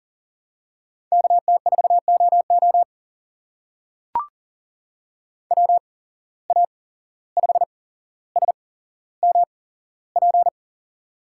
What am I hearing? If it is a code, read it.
Morse code, heard as a keyed tone, about 38 wpm, spaced as for 16 wpm, 700 Hz. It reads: KT4OO WA5SMP
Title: USA Call Signs - 8x Character Spacing - 40wpm